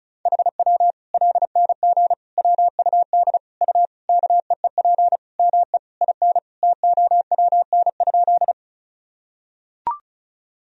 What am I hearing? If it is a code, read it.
Morse code, heard as a keyed tone, about 34 wpm, 700 Hz. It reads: HW LNG WUD U KEEP ME IN TOWN?
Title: Sentences from Top 500 Words and Abbreviations 35wpm